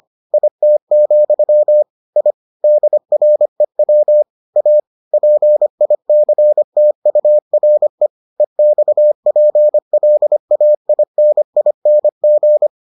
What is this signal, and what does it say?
Morse code, heard as a keyed tone, about 25 wpm, 600 Hz. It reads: IT, I DREW A PICTURE EXPLAINING